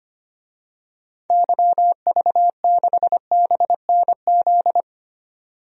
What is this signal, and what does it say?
Morse code, heard as a keyed tone, about 25 wpm, 700 Hz. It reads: YV6BNZ
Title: DXCC-100 Call Signs - 6 Characters - 25wpm